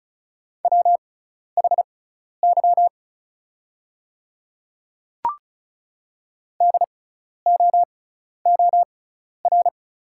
Morse code, heard as a keyed tone, about 36 wpm, spaced as for 12 wpm, 700 Hz. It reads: WHY DOOR
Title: 6x Character Spacing - Top 300 Words - 35wpm